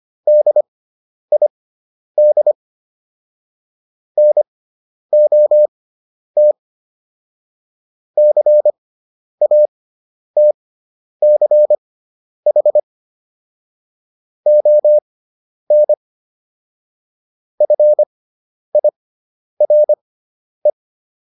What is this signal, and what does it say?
Morse code, heard as a keyed tone, about 25 wpm, spaced as for 10 wpm, 600 Hz. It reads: DID NOT CATCH ON FIRE